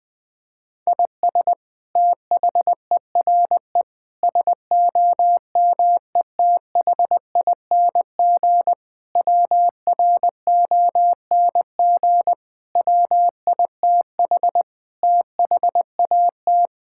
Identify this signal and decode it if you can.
Morse code, heard as a keyed tone, about 20 wpm, 700 Hz. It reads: IS THERE SOMETHING WRONG WITH THAT
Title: Sentences from Top 1000 Words - Review Part 6 - 20wpm